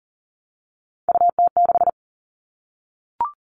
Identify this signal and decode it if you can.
Morse code, heard as a keyed tone, about 40 wpm, 700 Hz. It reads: UT6
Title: All Cut Numbers - 3-Digit 40wpm